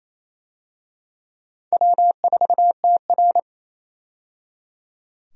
Morse code, heard as a keyed tone, about 28 wpm, 700 Hz. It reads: W4TR
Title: USA Call Signs - 4 Characters - 28wpm